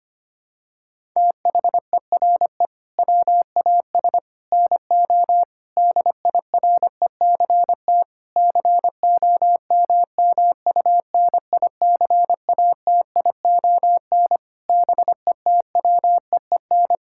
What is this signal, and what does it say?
Morse code, heard as a keyed tone, about 25 wpm, 700 Hz. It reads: THERE WAS NO DIRECT COMMUNICATION BETWEEN
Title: Sentences from Top 1000 Words - Review Part 6 - 25wpm